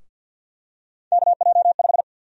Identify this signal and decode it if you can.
Morse code, heard as a keyed tone, about 49 wpm, 700 Hz. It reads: KJ5
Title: USA Call Sign Prefixes - Repeated 2x - 50wpm